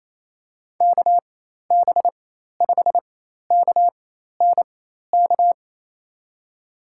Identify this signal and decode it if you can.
Morse code, heard as a keyed tone, about 27 wpm, spaced as for 13 wpm, 700 Hz. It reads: KB5KNK
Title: USA Call Signs - 6 Characters - 4x Character Spacing - 28wpm